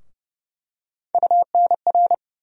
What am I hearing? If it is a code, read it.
Morse code, heard as a keyed tone, about 30 wpm, 700 Hz. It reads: UNR
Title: USA Call Sign Postfixes - 3 Letters 30wpm